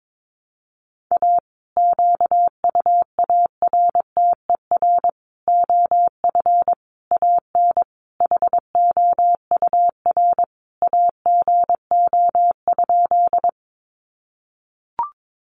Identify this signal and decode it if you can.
Morse code, heard as a keyed tone, about 22 wpm, 700 Hz. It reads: A QUARTER OF AN HOUR AGO?
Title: Sentences from Top 1300 Words 22wpm